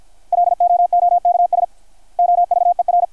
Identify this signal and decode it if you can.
Morse code, heard as a keyed tone, about 51 wpm, 660 Hz. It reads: LOOKS OVER
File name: archived / CRSnet